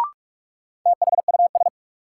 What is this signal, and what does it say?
Morse code, heard as a keyed tone, about 42 wpm, 700 Hz. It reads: THUS
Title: Sentences from Top 800 Words - New Word Drill - 45wpm